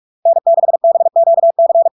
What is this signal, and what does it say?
Morse code, heard as a keyed tone, about 46 wpm, 670 Hz. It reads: N6BQC
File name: LARK-FD-2022-Log-in-CW-45wpm-670-Hz